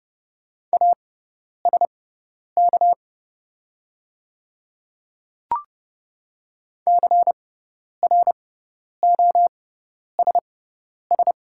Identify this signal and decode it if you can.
Morse code, heard as a keyed tone, about 30 wpm, spaced as for 10 wpm, 700 Hz. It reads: ASK CROSS
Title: 6x Character Spacing - Top 300 Words - 30wpm